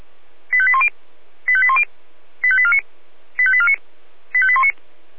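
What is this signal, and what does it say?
Five-tone selective call (ZVEI), heard as quick runs of five tones, tones of about 70 ms. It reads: 85810 85810 85849 85849 85819
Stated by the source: ZVEI five-tone selective call 85875 received by SDR.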